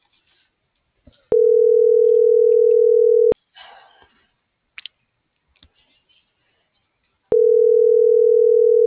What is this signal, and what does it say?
Ringback tone (North America), 440 + 480 Hz.